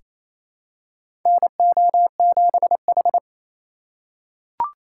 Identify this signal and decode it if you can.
Morse code, heard as a keyed tone, about 28 wpm, 700 Hz. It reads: NO7H
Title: USA Call Signs - 4 Characters - 4x Character Spacing - 28wpm